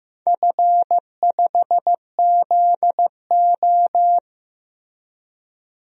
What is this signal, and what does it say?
Morse code, heard as a keyed tone, about 15 wpm, 700 Hz. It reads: F5ZO
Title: DXCC-100 Call Signs - 4 Characters - 15wpm